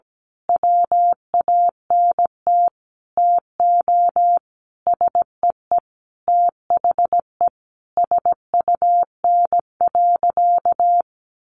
Morse code, heard as a keyed tone, about 17 wpm, 700 Hz. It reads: WANT TO SEE THE SUN.
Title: Easy Phrases 17wpm